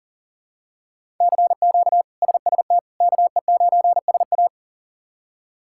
Morse code, heard as a keyed tone, about 37 wpm, 700 Hz. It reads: CQ SST KE9SA
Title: K1USN SST QSO Practice - 40wpm